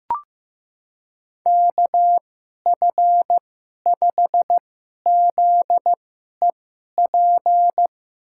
Morse code, heard as a keyed tone, about 15 wpm, spaced as for 11 wpm, 700 Hz. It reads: KF5ZEP
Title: USA Call Signs - 6 Characters - 2x Character Spacing - 15wpm